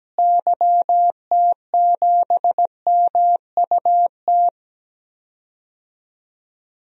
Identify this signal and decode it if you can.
Morse code, heard as a keyed tone, about 17 wpm, 700 Hz. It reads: YT7MUT
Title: DXCC-100 Call Signs - 6 Characters - Repeated 3x - 17wpm